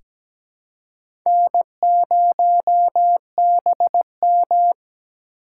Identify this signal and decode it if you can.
Morse code, heard as a keyed tone, about 17 wpm, 700 Hz. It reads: N0BM
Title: USA Call Signs - 4 Characters - 2x Character Spacing - 17wpm